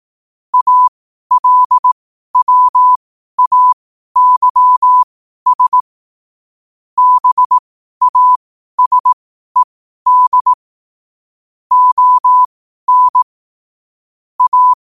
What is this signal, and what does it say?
Morse code, heard as a keyed tone, about 18 wpm, spaced as for 12 wpm, 1000 Hz. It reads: ALWAYS BASED ON A
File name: qotdfiles / 12wpm